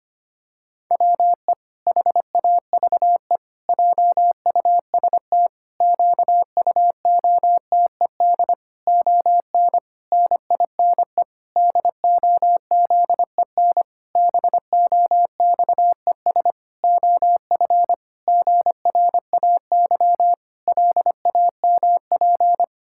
Morse code, heard as a keyed tone, about 25 wpm, 700 Hz. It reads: WE HAVE JUST QUOTED ON NINE DOZEN BOXES OF GRAY LAMP
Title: Pangram Challenge v2 - 25wpm